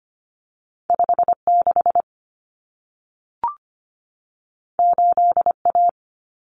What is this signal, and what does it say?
Morse code, heard as a keyed tone, about 25 wpm, 700 Hz. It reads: 56 8A
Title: Common Cut Numbers - 2-Digit 25wpm